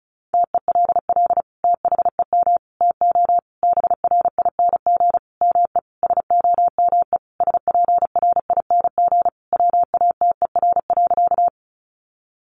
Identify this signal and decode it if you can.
Morse code, heard as a keyed tone, about 34 wpm, 700 Hz. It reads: TELL THEM TO BRING ME SOME SPRING WATER.
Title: Sentences from Top 1200 Words 35wpm